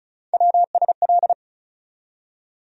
Morse code, heard as a keyed tone, about 34 wpm, 700 Hz. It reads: WSL
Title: USA Call Sign Postfixes - 3 Letters 35wpm